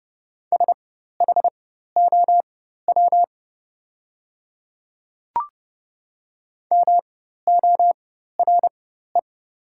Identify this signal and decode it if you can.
Morse code, heard as a keyed tone, about 30 wpm, spaced as for 14 wpm, 700 Hz. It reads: SHOW MORE
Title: 4x Character Spacing - Top 200 Words - 30wpm